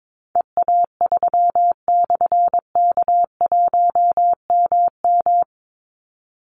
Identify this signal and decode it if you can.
Morse code, heard as a keyed tone, about 22 wpm, 700 Hz. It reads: EA3/K1MM
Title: Call Signs - 22wpm